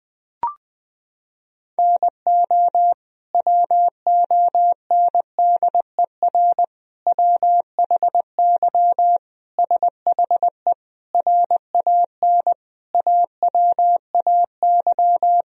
Morse code, heard as a keyed tone, about 20 wpm, 700 Hz. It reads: NO WONDER WHY SHE RAN AWAY
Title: Sentences from Top 1000 Words - Review Part 3 - 20wpm